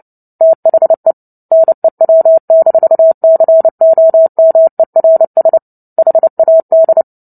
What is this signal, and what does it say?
Morse code, heard as a keyed tone, about 30 wpm, 650 Hz. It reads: THE NEW-COMERS HAD
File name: Book Two / 03-Chapter 3 - 29 WPM